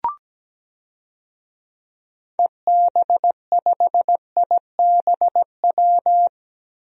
Morse code, heard as a keyed tone, about 17 wpm, 700 Hz. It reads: EB5IBW
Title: DXCC-100 Call Signs - 6 Characters - 17wpm